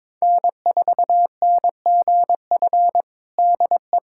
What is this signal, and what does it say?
Morse code, heard as a keyed tone, about 22 wpm, 700 Hz. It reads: N4NGF DE
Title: USA Callsign DE Callsign - 5 Characters - 22wpm